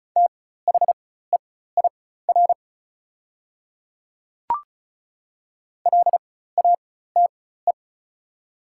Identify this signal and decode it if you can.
Morse code, heard as a keyed tone, about 36 wpm, spaced as for 16 wpm, 700 Hz. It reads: THEIR LATE
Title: 4x Character Spacing - Top 300 Words - 35wpm